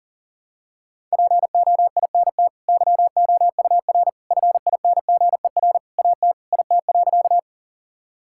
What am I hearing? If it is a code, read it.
Morse code, heard as a keyed tone, about 40 wpm, 700 Hz. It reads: POINT YOUR FINGER AT IT.
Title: Sentences from Top 1600 Words 40wpm